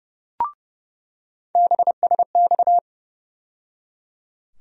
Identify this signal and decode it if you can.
Morse code, heard as a keyed tone, about 31 wpm, 700 Hz. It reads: BSX
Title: USA Call Sign Postfixes - 3 Letters 30wpm